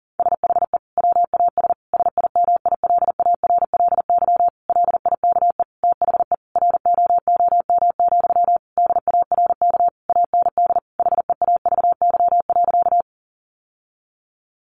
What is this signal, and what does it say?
Morse code, heard as a keyed tone, about 40 wpm, 700 Hz. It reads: SHE WAS SIMILARLY LIKE THE ROOM, DARK AND HEAVY.
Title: Sentences from Top 1400 Words 40wpm